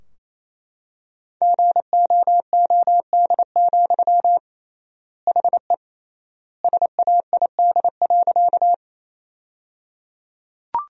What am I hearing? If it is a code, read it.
Morse code, heard as a keyed tone, about 28 wpm, 700 Hz. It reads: GOOD, HE SAID.
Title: 3x Word-Spacing - Sentences from Top 200 Words - 28wpm